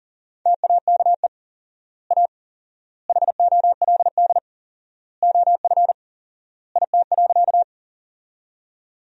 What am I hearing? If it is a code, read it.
Morse code, heard as a keyed tone, about 37 wpm, 700 Hz. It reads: TAKE A HOLD OF IT.
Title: 4x Word-Spacing - Sentences from Top 400 Words - 40wpm